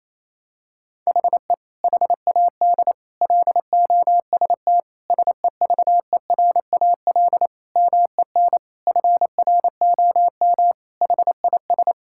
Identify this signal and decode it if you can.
Morse code, heard as a keyed tone, about 28 wpm, 700 Hz. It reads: HE HAD LOST SEVERAL MEN FROM HIS
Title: Sentences from Top 1600 Words 28wpm